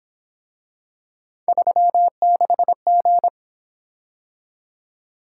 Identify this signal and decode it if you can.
Morse code, heard as a keyed tone, about 27 wpm, 700 Hz. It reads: 36G
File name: Speed-Racing - Three Letter-Number Combinations - 22wpm